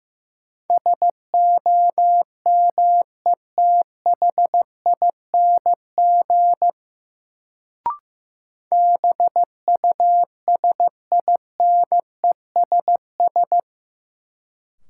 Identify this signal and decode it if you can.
Morse code, heard as a keyed tone, about 15 wpm, 700 Hz. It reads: SOMETHING BUSINESS
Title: Sentences from Top 600 Words - New Word Drill - 15wpm